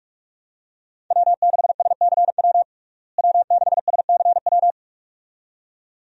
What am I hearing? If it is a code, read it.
Morse code, heard as a keyed tone, about 46 wpm, 700 Hz. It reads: W6SCW W6SCW
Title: USA Call Signs - 5 Characters - Repeated 2x - 45wpm